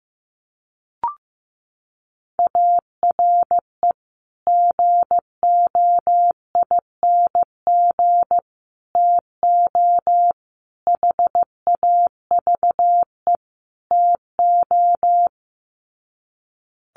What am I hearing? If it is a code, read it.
Morse code, heard as a keyed tone, about 15 wpm, 700 Hz. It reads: ARE GOING TO HAVE TO
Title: Sets of 5 Words 15wpm